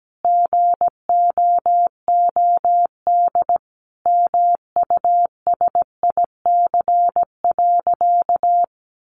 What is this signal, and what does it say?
Morse code, heard as a keyed tone, about 17 wpm, 700 Hz. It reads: GOOD MUSIC.
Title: Easy Phrases 17wpm